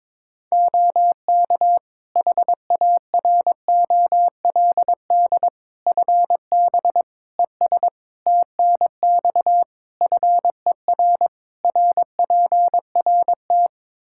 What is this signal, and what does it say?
Morse code, heard as a keyed tone, about 22 wpm, 700 Hz. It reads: OK HAROLD FB ES TNX FER RPRT
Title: Practice QSO - Exchange 2 - Info - 22wpm